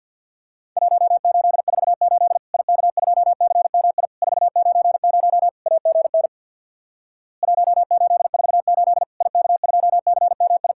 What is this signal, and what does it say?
Morse code, heard as a keyed tone, about 50 wpm, 700 Hz. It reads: 1848 IK2CMI 490 AGN 1848 IK2CMI
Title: Worked All Europe DX Contest - QTC Exchange with Repeats - 50wpm